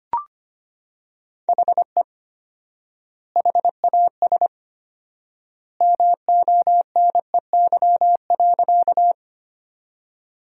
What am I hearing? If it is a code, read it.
Morse code, heard as a keyed tone, about 25 wpm, 700 Hz. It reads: HE HAS MONEY.
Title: 4x Word-Spacing - Sentences from Top 400 Words - 25wpm